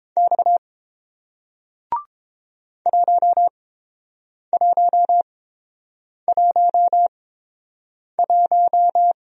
Morse code, heard as a keyed tone, about 29 wpm, spaced as for 7 wpm, 700 Hz. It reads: X 1111